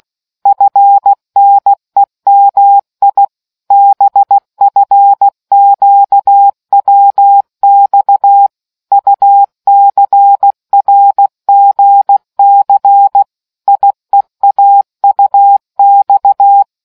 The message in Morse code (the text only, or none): FNEMI BFQWX UCRGC IEAUX